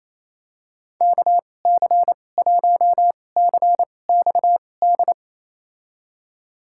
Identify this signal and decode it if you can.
Morse code, heard as a keyed tone, about 28 wpm, spaced as for 20 wpm, 690 Hz. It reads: KC1CXD
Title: USA Call Signs - 6 Characters - 2x Character Spacing - 28wpm